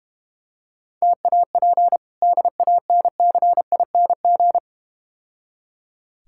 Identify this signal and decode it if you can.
Morse code, heard as a keyed tone, about 30 wpm, 690 Hz. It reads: TAP DANCING
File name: practice / 25wpm / Difficult Words - Speed-Racing with Hints - 25wpm